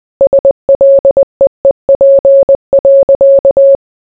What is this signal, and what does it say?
Morse code, heard as a keyed tone, about 20 wpm, 550 Hz. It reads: SLEEP.